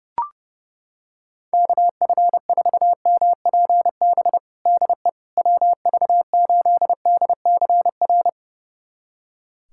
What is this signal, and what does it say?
Morse code, heard as a keyed tone, about 32 wpm, 690 Hz. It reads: KF4MPB DE WV8DCR
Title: USA Callsign DE Callsign 30wpm